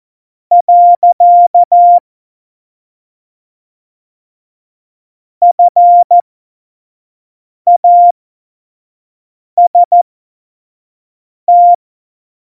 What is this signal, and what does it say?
Morse code, heard as a keyed tone, about 14 wpm, spaced as for 5 wpm, 700 Hz. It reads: . FAST
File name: Morsekurs / Witze / Witz 15. Mai 2018, 25 F70